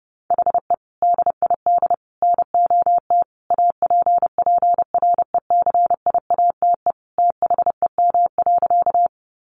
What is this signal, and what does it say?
Morse code, heard as a keyed tone, about 28 wpm, 700 Hz. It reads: HE DID NOT APPRECIATE THEM.
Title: Sentences from Top 1300 Words 30wpm